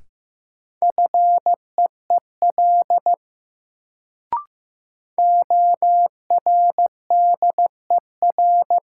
Morse code, heard as a keyed tone, about 15 wpm, 700 Hz. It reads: FEEL ORDER
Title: Sentences from Top 400 Words - New Word Drill - 15wpm